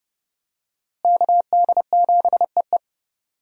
Morse code, heard as a keyed tone, about 30 wpm, 700 Hz. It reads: KD7EE
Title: USA Call Signs - 5 Characters - Repeated 2x - 30wpm